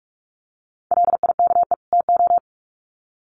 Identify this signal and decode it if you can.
Morse code, heard as a keyed tone, about 44 wpm, 700 Hz. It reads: LIKE TO
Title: Sets of 2 Words 45wpm